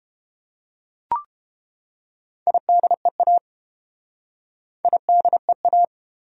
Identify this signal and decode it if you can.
Morse code, heard as a keyed tone, about 30 wpm, 700 Hz. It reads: IDEA IDEA